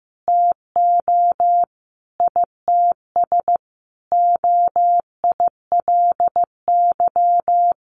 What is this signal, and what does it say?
Morse code, heard as a keyed tone, about 15 wpm, 700 Hz. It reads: TO ITS OILY
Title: A Sound Of Thunder 15wpm